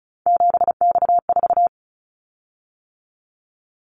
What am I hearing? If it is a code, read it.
Morse code, heard as a keyed tone, about 36 wpm, 700 Hz. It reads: 7X4
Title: Callsign Prefixes for Top 100 DXCC - 35wpm